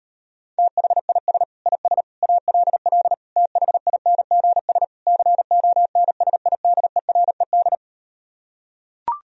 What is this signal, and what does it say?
Morse code, heard as a keyed tone, about 38 wpm, 700 Hz. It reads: THIS IS ALL THINGS CONSIDERED